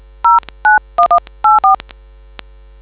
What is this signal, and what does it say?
DTMF, heard as telephone touch tones, tones of about 160 ms. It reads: 09184